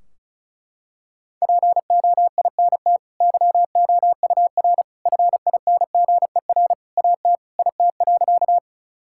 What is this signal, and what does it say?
Morse code, heard as a keyed tone, about 36 wpm, 700 Hz. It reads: POINT YOUR FINGER AT IT.
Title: Sentences from Top 1600 Words 35wpm